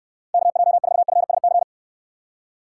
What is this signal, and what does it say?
Morse code, heard as a keyed tone, about 66 wpm, 700 Hz. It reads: KJ4FSC